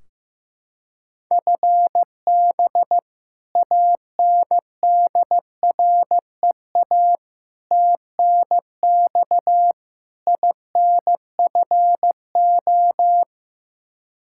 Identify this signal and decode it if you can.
Morse code, heard as a keyed tone, about 15 wpm, 700 Hz. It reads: FB ANDREA TNX INFO
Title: Practice QSO - Exchange 3 - More Info - 15wpm